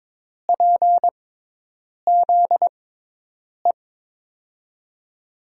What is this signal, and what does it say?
Morse code, heard as a keyed tone, about 21 wpm, spaced as for 8 wpm, 700 Hz. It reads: PZE
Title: USA Call Signs - 5 Characters - 6x Character Spacing - 22wpm